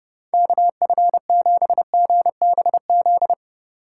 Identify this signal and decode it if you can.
Morse code, heard as a keyed tone, about 31 wpm, 700 Hz. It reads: KF7GBZ
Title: USA Call Signs - 6 Characters - 3x Character Spacing - 30wpm